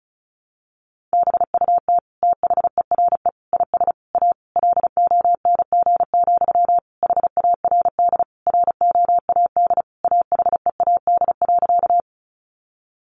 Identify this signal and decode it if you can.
Morse code, heard as a keyed tone, about 34 wpm, 700 Hz. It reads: BUT THERE IS A LONG, HARD ROAD AHEAD.
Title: Sentences from Top 1100 Words 35wpm